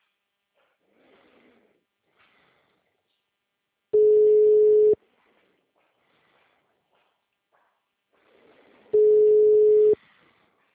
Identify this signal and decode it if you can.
Ringback tone (Europe, 425 Hz).